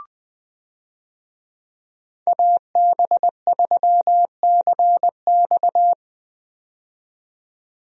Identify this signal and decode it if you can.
Morse code, heard as a keyed tone, about 20 wpm, 700 Hz. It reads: AB3CX
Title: USA Call Signs - 5 Characters - 20wpm